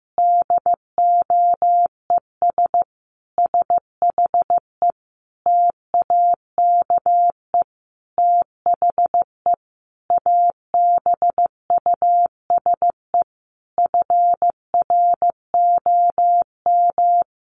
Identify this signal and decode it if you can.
Morse code, heard as a keyed tone, about 15 wpm, 700 Hz. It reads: DOES SHE TAKE THE ABUSE FROM
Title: Sentences from Top 1300 Words 15wpm